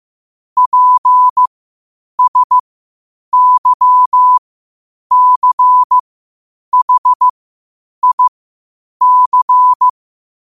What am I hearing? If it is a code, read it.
Morse code, heard as a keyed tone, about 15 wpm, spaced as for 8 wpm, 1000 Hz. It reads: PSYCHIC
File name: qotdfiles / 8wpm